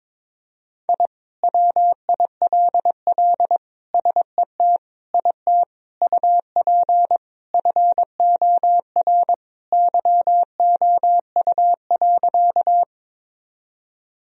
Morse code, heard as a keyed tone, about 22 wpm, 700 Hz. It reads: I WILL SET IT UP FOR YOU.
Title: Sentences from Top 1000 Words - Review Part 1 - 22wpm